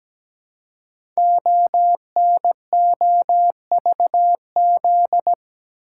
Morse code, heard as a keyed tone, about 17 wpm, 700 Hz. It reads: ONOVZ
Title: Code Groups - 5 Letters - 17wpm